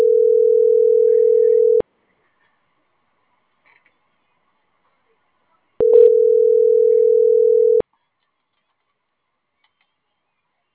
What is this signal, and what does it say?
Ringback tone (North America), 440 + 480 Hz.
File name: CENTRAL GENERAL / out yazna.cozumel